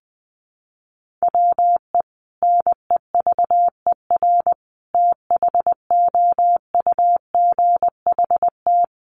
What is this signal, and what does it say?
Morse code, heard as a keyed tone, about 20 wpm, 700 Hz. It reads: WE NEVER THOUGHT
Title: Sentences from Top 300 Words 20wpm